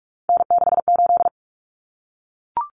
Morse code, heard as a keyed tone, about 44 wpm, 700 Hz. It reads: N68
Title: Common Cut Numbers - 3-Digit 45wpm